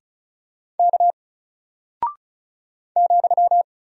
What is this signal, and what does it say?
Morse code, heard as a keyed tone, about 35 wpm, 700 Hz. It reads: K ,
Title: Single Character (Letter:Number:Punctuation:Prosign) 35wpm